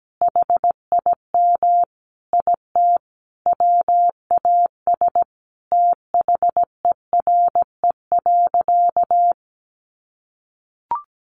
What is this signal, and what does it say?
Morse code, heard as a keyed tone, about 17 wpm, 700 Hz. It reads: HIM IT WAS THERE.
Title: A Sound Of Thunder 17wpm